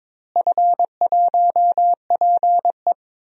Morse code, heard as a keyed tone, about 22 wpm, 700 Hz. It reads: F1PE
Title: DXCC-100 Call Signs - 4 Characters - Repeated 3x - 22wpm